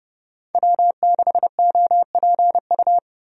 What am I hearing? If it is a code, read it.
Morse code, heard as a keyed tone, about 31 wpm, 700 Hz. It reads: W6OPU
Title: USA Call Signs - 5 Characters - Repeated 3x - 30wpm